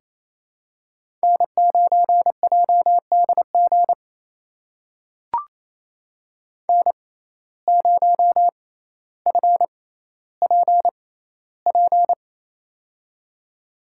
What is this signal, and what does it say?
Morse code, heard as a keyed tone, about 28 wpm, spaced as for 14 wpm, 700 Hz. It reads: N9JDG N0FPP
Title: USA Call Signs - 5 Characters - 6x Character Spacing - 28wpm